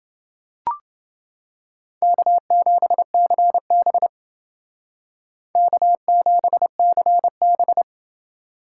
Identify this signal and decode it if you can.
Morse code, heard as a keyed tone, about 28 wpm, 700 Hz. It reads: K7CB K7CB